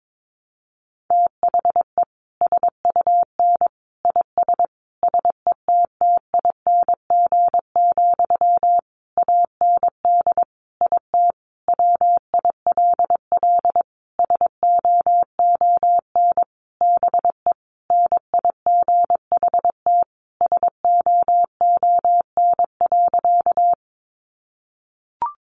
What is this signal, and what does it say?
Morse code, heard as a keyed tone, about 22 wpm, 700 Hz. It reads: THE SUN IS SETTING, AND IT WILL SOON BE NIGHT SOON.
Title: Sentences from Top 1300 Words 22wpm